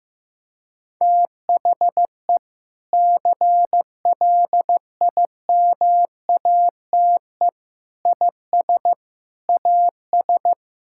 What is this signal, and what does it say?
Morse code, heard as a keyed tone, about 16 wpm, 700 Hz. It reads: THE CLIMATE IS AS
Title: Sentences from Top 1600 Words 15wpm